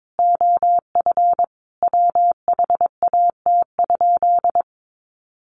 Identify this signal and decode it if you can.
Morse code, heard as a keyed tone, about 22 wpm, 700 Hz. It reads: OF WHAT?
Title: Sentences from Top 1500 Words 22wpm